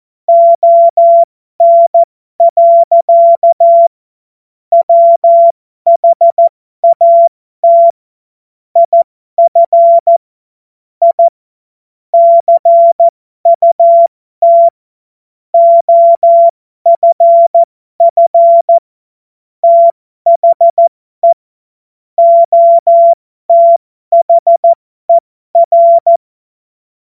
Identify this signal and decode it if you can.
Morse code, heard as a keyed tone, about 14 wpm, spaced as for 12 wpm, 680 Hz. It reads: ON. WHAT IF I CUT OFF THE OTHER